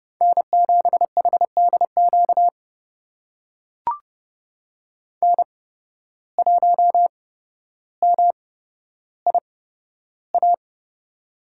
Morse code, heard as a keyed tone, about 30 wpm, spaced as for 13 wpm, 700 Hz. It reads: N7HDQ N1MIA
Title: USA Call Signs - 5 Characters - 8x Character Spacing - 30wpm